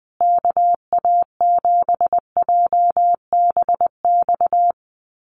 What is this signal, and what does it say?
Morse code, heard as a keyed tone, about 20 wpm, 700 Hz. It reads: KA7JBX
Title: USA Call Signs Repeated 3x 20wpm